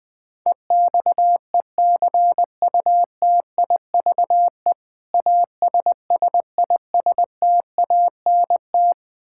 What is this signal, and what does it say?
Morse code, heard as a keyed tone, about 20 wpm, 700 Hz. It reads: EXECUTIVE ASSISTANT
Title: Practice QSO - Exchange 3 - More Info - 20wpm